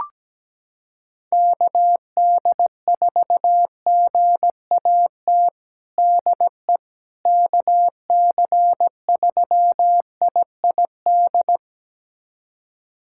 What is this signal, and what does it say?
Morse code, heard as a keyed tone, about 17 wpm, 700 Hz. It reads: KD4GAT DE KC3IID
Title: USA Callsign DE Callsign - 6 Characters - 17wpm